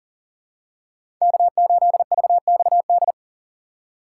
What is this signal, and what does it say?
Morse code, heard as a keyed tone, about 40 wpm, 700 Hz. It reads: K8VXD
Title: USA Call Signs - 5 Characters - 4x Character Spacing - 40wpm